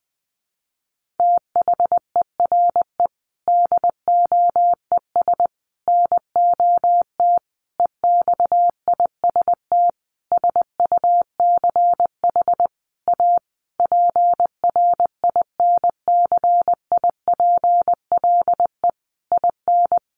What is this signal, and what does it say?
Morse code, heard as a keyed tone, about 20 wpm, 700 Hz. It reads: THERE DOES NOT EXIST SUCH A PRINCIPLE IN
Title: Sentences from Top 1400 Words 20wpm